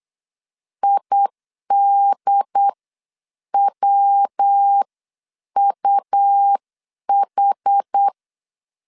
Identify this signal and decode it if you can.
Morse code, heard as a keyed tone, about 8 wpm, spaced as for 7 wpm, 800 Hz. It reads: IDWUH